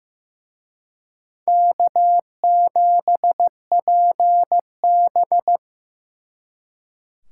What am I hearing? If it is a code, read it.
Morse code, heard as a keyed tone, about 18 wpm, 700 Hz. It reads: K7PB
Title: Speed Racing - USA 1x2 Call Signs 15wpm